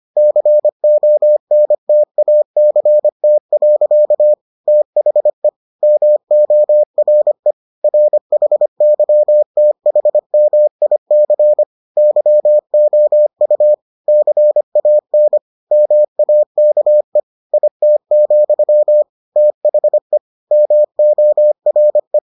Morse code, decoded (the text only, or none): CONTACT. THE MORE RHYTHMIC YOU CAN MAKE IT, THE MORE